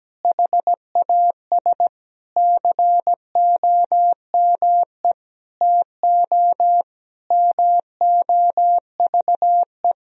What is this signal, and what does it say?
Morse code, heard as a keyed tone, about 17 wpm, 700 Hz. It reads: HAS COME TO MOVE
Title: Sentences from Top 1000 Words - Review Part 1 - 17wpm